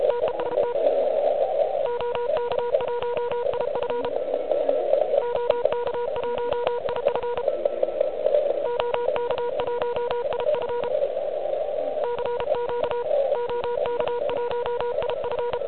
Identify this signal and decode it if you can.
Morse code, heard as a keyed tone, about 32 wpm, 960 Hz, with noise. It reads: TEST OK1IF OK1IF OK1IF CQ OK1IF
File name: ARCHIV / QCX / QCX Pileup